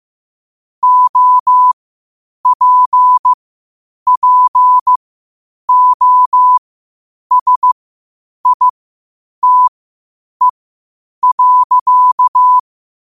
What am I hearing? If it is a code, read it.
Morse code, heard as a keyed tone, about 15 wpm, spaced as for 8 wpm, 1000 Hz. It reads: OPPOSITE.
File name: qotdfiles / 8wpm